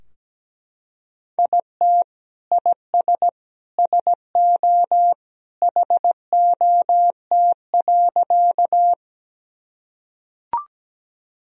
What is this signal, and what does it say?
Morse code, heard as a keyed tone, about 17 wpm, 700 Hz. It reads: IT IS SO HOT.
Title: Sentences from Top 1000 Words - Review Part 3 - 17wpm